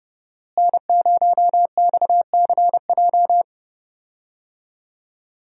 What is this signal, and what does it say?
Morse code, heard as a keyed tone, about 30 wpm, 700 Hz. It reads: N0XCJ
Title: USA Call Signs - 5 Characters - Repeated 3x - 30wpm